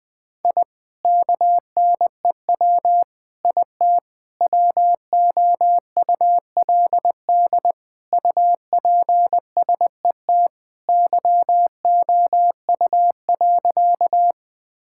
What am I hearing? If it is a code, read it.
Morse code, heard as a keyed tone, about 20 wpm, 700 Hz. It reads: I KNEW IT WOULD UPSET YOU.
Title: Sentences from Top 1000 Words - Review Part 5 - 20wpm